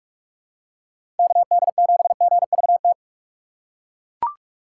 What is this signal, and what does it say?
Morse code, heard as a keyed tone, about 41 wpm, 700 Hz. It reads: KD7GVT
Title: USA Call Signs - 6 Characters - 4x Character Spacing - 45wpm